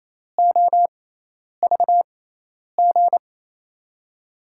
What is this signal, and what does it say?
Morse code, heard as a keyed tone, about 28 wpm, spaced as for 10 wpm, 700 Hz. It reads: OVG
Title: USA Call Signs - 6 Characters - 6x Character Spacing - 28wpm